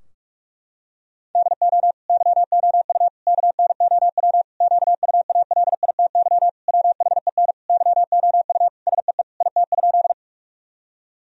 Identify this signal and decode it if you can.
Morse code, heard as a keyed tone, about 41 wpm, 700 Hz. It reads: DO YOU KNOW QUALITY WHEN YOU SEE IT?
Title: Sentences from Top 1000 Words - Review Part 4 - 45wpm